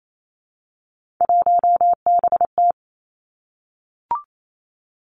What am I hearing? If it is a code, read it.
Morse code, heard as a keyed tone, about 28 wpm, 700 Hz. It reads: 1BT
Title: Three Challenging Letter-Number Combinations 28wpm